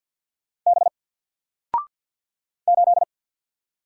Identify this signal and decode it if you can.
Morse code, heard as a keyed tone, about 48 wpm, 700 Hz. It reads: B 8
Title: Single Letter-Number Alt 50wpm